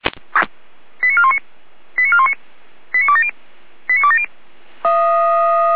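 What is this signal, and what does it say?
Five-tone selective call (ZVEI), heard as quick runs of five tones, tones of about 80 ms. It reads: 89419 89419 89270 89270
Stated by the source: ZVEI five-tone selective call 89270 received via SDR.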